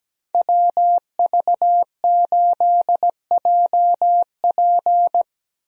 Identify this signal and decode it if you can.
Morse code, heard as a keyed tone, about 17 wpm, 700 Hz. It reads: WV8JP
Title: USA Call Signs - 5 Characters - Repeated 3x - 17wpm